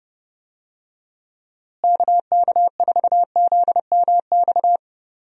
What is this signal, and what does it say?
Morse code, heard as a keyed tone, about 30 wpm, 700 Hz. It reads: KK4ZMX
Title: USA Call Signs - 6 Characters - Repeated 2x - 30wpm